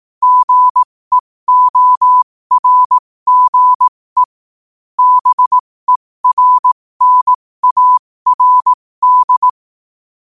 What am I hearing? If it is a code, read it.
Morse code, heard as a keyed tone, about 18 wpm, spaced as for 15 wpm, 1000 Hz. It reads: GEORGE BERNARD
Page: New quotes every day in morse code at 15 Words per minute.